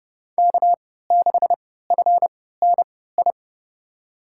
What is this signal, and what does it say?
Morse code, heard as a keyed tone, about 30 wpm, spaced as for 17 wpm, 700 Hz. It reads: K6FNI
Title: USA Call Signs - 3x Character Spacing - 30wpm